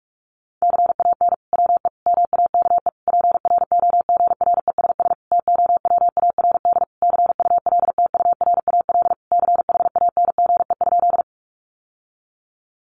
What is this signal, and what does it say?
Morse code, heard as a keyed tone, about 44 wpm, 700 Hz. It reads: CAN WE MAKE PROGRESS TOWARD CULTURAL CHANGE?
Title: Sentences from Top 1500 Words 45wpm